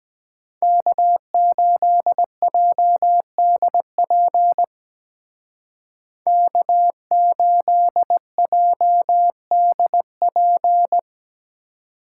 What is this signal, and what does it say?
Morse code, heard as a keyed tone, about 20 wpm, 700 Hz. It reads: K8JDP K8JDP
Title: Speed Racing - USA Call Signs 17wpm